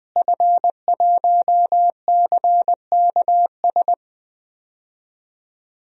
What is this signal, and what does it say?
Morse code, heard as a keyed tone, about 21 wpm, 700 Hz. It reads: F1CKS
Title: DXCC-100 Call Signs - 5 Characters - Repeated 3x - 20wpm